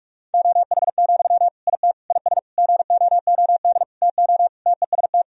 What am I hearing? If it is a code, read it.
Morse code, heard as a keyed tone, about 46 wpm, 700 Hz. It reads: OH, IT IS GOOD TO TEST
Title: Sentences from Top 1000 Words - Review Part 3 - 45wpm